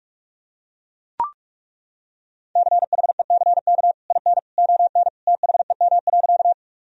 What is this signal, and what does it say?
Morse code, heard as a keyed tone, about 42 wpm, 700 Hz. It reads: CHECK IN ON THEM.
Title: Sentences from Top 1000 Words - Review Part 3 - 45wpm